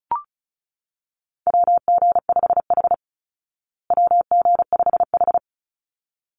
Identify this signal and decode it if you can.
Morse code, heard as a keyed tone, about 34 wpm, 700 Hz. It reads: WG5H WG5H
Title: Call Signs Repeated 3x - 35wpm